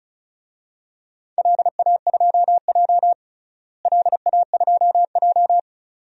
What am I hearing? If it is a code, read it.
Morse code, heard as a keyed tone, about 34 wpm, 700 Hz. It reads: LA2J LA2J
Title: DXCC-100 Call Signs - 4 Characters - Repeated 2x - 35wpm